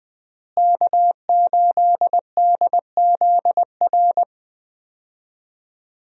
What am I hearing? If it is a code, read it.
Morse code, heard as a keyed tone, about 20 wpm, 700 Hz. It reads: K8DZR
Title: USA Call Signs - 5 Characters - Repeated 2x - 20wpm